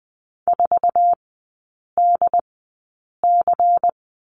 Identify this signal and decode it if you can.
Morse code, heard as a keyed tone, about 20 wpm, spaced as for 8 wpm, 700 Hz. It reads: 4DC
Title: 093 Lesson 16e - Letter c - Three Characters 20wpm